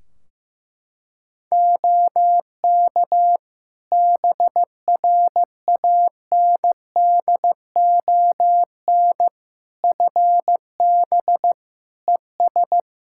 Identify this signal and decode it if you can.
Morse code, heard as a keyed tone, about 15 wpm, 700 Hz. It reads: OK BRANDON FB ES
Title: K1USN SST QSO Practice 15wpm